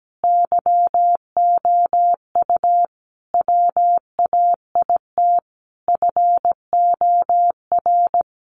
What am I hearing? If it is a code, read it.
Morse code, heard as a keyed tone, about 17 wpm, 700 Hz. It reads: YOU WAIT FOR
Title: Sentences from Top 900 Words 17wpm